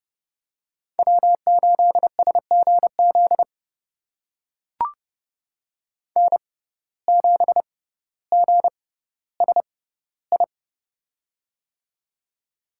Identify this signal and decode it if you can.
Morse code, heard as a keyed tone, about 30 wpm, spaced as for 15 wpm, 700 Hz. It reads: W8SGZ N7GSI
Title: USA Call Signs - 5 Characters - 6x Character Spacing - 30wpm